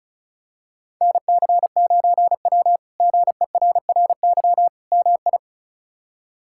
Morse code, heard as a keyed tone, about 36 wpm, 700 Hz. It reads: NC9W GERRY MI
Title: K1USN SST QSO Practice - 35wpm